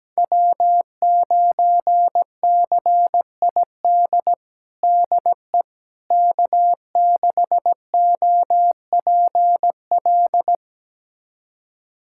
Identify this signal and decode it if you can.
Morse code, heard as a keyed tone, about 17 wpm, 700 Hz. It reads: W9CID DE K6OPL
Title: USA Callsign DE Callsign - 5 Characters - 17wpm